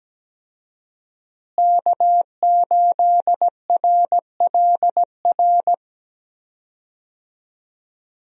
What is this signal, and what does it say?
Morse code, heard as a keyed tone, about 17 wpm, 700 Hz. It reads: K8RLR
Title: USA Call Signs - 5 Characters - Repeated 3x - 17wpm